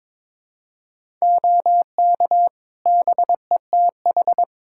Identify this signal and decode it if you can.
Morse code, heard as a keyed tone, about 22 wpm, 700 Hz. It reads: OK BETH
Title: Practice QSO - Exchange 3 - More Info - 22wpm